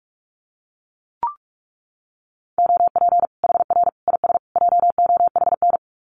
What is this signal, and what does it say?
Morse code, heard as a keyed tone, about 44 wpm, 700 Hz. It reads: OP HR IS JOHN
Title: Common QSO Phrases 45wpm